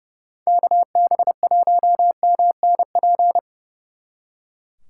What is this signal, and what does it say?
Morse code, heard as a keyed tone, about 38 wpm, 700 Hz. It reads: KB1MNP